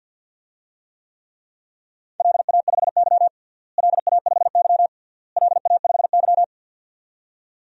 Morse code, heard as a keyed tone, about 52 wpm, 700 Hz. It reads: LA5Y LA5Y LA5Y
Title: DXCC-100 Call Signs - 4 Characters - Repeated 3x - 50wpm